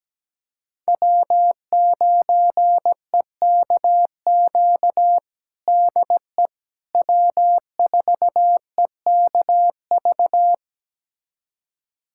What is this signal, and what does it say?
Morse code, heard as a keyed tone, about 17 wpm, 700 Hz. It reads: W9EKQ DE W4EKV
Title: USA Callsign DE Callsign - 5 Characters - 17wpm